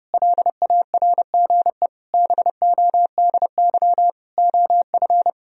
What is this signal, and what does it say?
Morse code, heard as a keyed tone, about 30 wpm, 700 Hz. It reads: LARGE BODY OF
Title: Sentences from Top 1000 Words - Review Part 4 - 30wpm